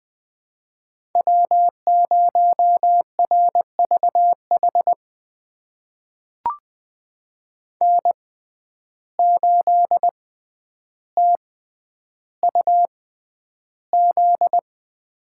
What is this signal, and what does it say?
Morse code, heard as a keyed tone, about 20 wpm, spaced as for 10 wpm, 700 Hz. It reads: W0RVH N8TUZ
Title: USA Call Signs - 5 Characters - 6x Character Spacing - 20wpm